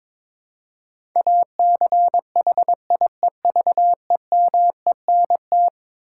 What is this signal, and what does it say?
Morse code, heard as a keyed tone, about 22 wpm, 700 Hz. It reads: ACHIEVEMENT
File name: practice / 22wpm / Eleven Letter Words - 22wpm